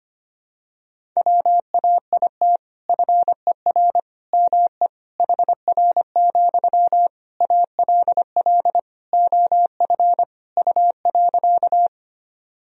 Morse code, heard as a keyed tone, about 25 wpm, 700 Hz. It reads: WAIT FER ME HR, ALL OF U.
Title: Sentences from Top 500 Words and Abbreviations 25wpm